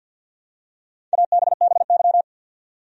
Morse code, heard as a keyed tone, about 49 wpm, 700 Hz. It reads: ABBY
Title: USA Common Names - Max 4 Letters - 50wpm